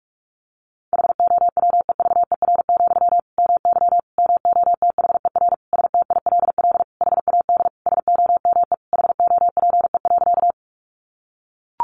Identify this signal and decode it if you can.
Morse code, heard as a keyed tone, about 44 wpm, 700 Hz. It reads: HOWEVER, MY MOTHER STILL HAD SOME HOPE.
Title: Sentences from Top 1000 Words 45wpm